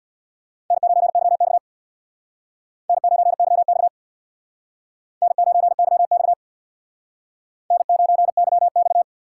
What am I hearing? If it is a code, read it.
Morse code, heard as a keyed tone, about 65 wpm, 700 Hz. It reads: N9YX N9YX N9YX N9YX